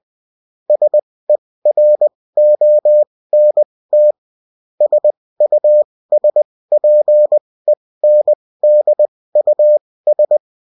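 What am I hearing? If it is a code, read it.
Morse code, heard as a keyed tone, about 20 wpm, spaced as for 16 wpm, 600 Hz. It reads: SERONT SUSPENDUS
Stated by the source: UFT bulletin sent in CW.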